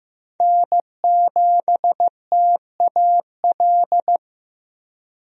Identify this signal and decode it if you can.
Morse code, heard as a keyed tone, about 15 wpm, 700 Hz. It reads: N7TAL
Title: USA Call Signs - 5 Characters - Repeated 2x - 15wpm